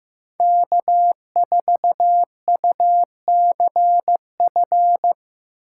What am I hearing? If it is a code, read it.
Morse code, heard as a keyed tone, about 18 wpm, 700 Hz. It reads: K4UCF
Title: Speed Racing - USA 1x3 Call Signs 15wpm